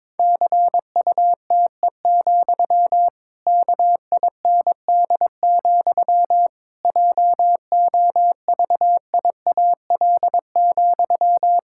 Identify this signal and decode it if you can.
Morse code, heard as a keyed tone, about 22 wpm, 700 Hz. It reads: CUTE, KIND, JOVIAL,
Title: Pangram Challenge v2 - 22wpm